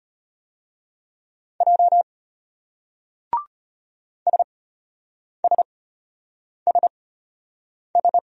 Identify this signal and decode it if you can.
Morse code, heard as a keyed tone, about 33 wpm, spaced as for 8 wpm, 700 Hz. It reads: J SSSS